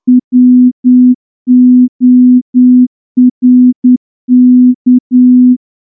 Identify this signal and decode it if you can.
Morse code, heard as a keyed tone, about 9 wpm, 250 Hz. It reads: WORK
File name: morse-training-10WPM